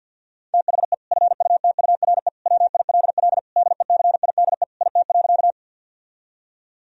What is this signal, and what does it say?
Morse code, heard as a keyed tone, about 50 wpm, 700 Hz. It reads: THE FUTURE WILL DECIDE IT.
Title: Sentences from Top 1000 Words - Review Part 3 - 50wpm